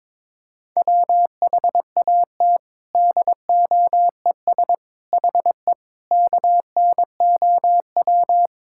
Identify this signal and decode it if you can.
Morse code, heard as a keyed tone, about 22 wpm, 700 Hz. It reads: WHAT DOES HE KNOW
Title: Sentences from Top 1600 Words 22wpm